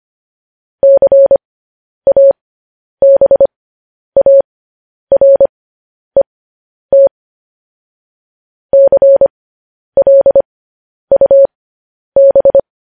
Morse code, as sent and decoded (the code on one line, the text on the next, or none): -.-. .- -... .- .-. - / -.-. .-.. ..- -...
CABARET CLUB